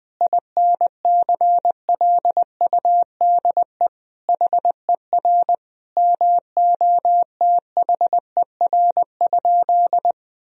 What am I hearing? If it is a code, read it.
Morse code, heard as a keyed tone, about 20 wpm, 700 Hz. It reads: INCLUDE HER MOTHER?
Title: Sentences from Top 1000 Words - Review Part 3 - 20wpm